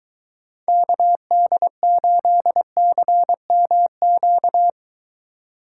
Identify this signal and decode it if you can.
Morse code, heard as a keyed tone, about 23 wpm, 700 Hz. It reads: KD8CMQ